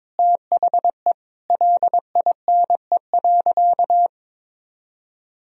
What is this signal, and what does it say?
Morse code, heard as a keyed tone, about 22 wpm, 700 Hz. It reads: THE LINE.
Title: Sentences from Top 1000 Words - Review Part 1 - 22wpm